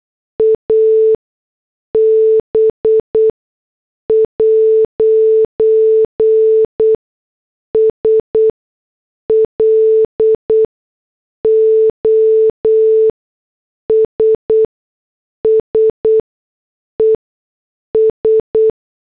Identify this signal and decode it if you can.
Morse code, heard as a keyed tone, about 8 wpm, spaced as for 6 wpm, 440 Hz. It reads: AB'S LOSSES
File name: morse / headlines